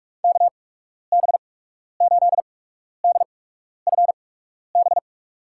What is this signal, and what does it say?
Morse code, heard as a keyed tone, about 47 wpm, spaced as for 12 wpm, 690 Hz. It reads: KB8DFB
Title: USA Call Signs - 8x Character Spacing - 45wpm